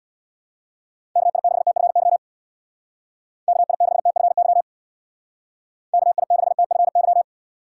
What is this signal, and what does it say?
Morse code, heard as a keyed tone, about 63 wpm, 700 Hz. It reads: KI6TFY KI6TFY KI6TFY